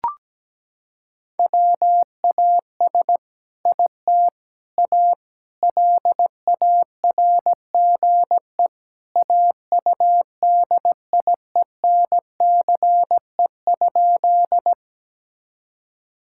Morse code, heard as a keyed tone, about 17 wpm, 700 Hz. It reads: WAS IT A LARGE AUDIENCE?
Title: Sentences from Top 1000 Words - Review Part 5 - 17wpm